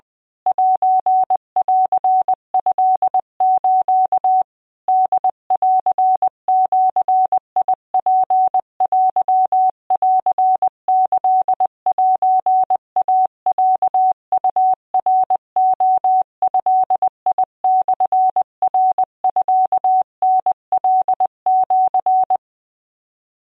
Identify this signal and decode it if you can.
Morse code (Japanese Wabun code), heard as a keyed tone, about 20 wpm, 750 Hz. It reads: セントス ホンジツテンキセイロウナレドモナミタカシ